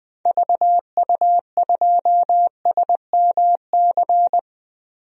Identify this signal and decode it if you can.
Morse code, heard as a keyed tone, about 20 wpm, 700 Hz. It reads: VU2SMC
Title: DXCC-100 Call Signs - 6 Characters - 20wpm